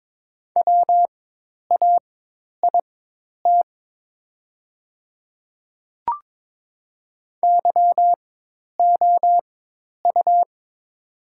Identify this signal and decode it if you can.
Morse code, heard as a keyed tone, about 22 wpm, spaced as for 10 wpm, 700 Hz. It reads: WAIT YOU
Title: 4x Character Spacing - Top 500 Words - 22wpm